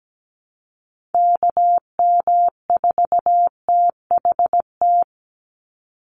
Morse code, heard as a keyed tone, about 17 wpm, 700 Hz. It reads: KM4THT
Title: USA Call Signs 17wpm